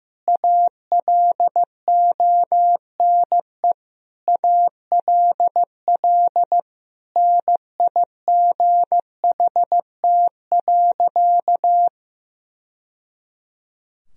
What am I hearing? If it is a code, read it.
Morse code, heard as a keyed tone, about 15 wpm, 700 Hz. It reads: ALONE ALL NIGHT.
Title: Sentences from Top 1000 Words - Review Part 2 - 15wpm